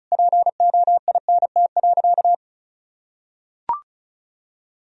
Morse code, heard as a keyed tone, about 36 wpm, 700 Hz. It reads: POINT.
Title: Sentences from Top 1000 Words - Review Part 4 - 35wpm